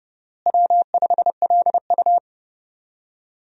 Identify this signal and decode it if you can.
Morse code, heard as a keyed tone, about 30 wpm, 700 Hz. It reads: W5LU
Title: USA Call Signs - 4 Characters - Repeated 3x - 30wpm